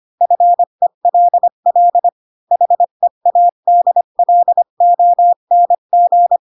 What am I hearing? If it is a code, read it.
Morse code, heard as a keyed tone, about 25 wpm, 700 Hz. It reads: FELL HEADLONG
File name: Pinocchio / Pinocchio ch7 20 25 ik1hge